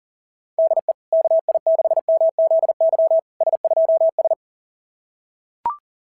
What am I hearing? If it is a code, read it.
Morse code, heard as a keyed tone, about 40 wpm, 690 Hz. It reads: DE KI6MZY S2S
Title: SOTA QSO Practice 40wpm